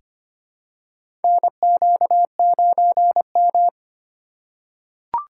Morse code, heard as a keyed tone, about 25 wpm, 700 Hz. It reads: NQ9M
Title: USA Call Signs - 4 Characters - 3x Character Spacing - 25wpm